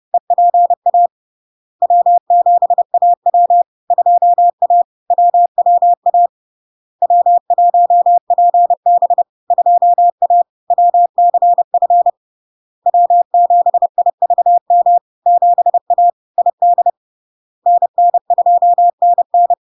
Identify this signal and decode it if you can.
Morse code, heard as a keyed tone, about 30 wpm, 700 Hz. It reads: EPA W7AW 2A WWA W1PB 2A WCF W7IVM 7A ID NN2NN
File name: LARK-FD-2022-Log-in-CW-30wpm-700-Hz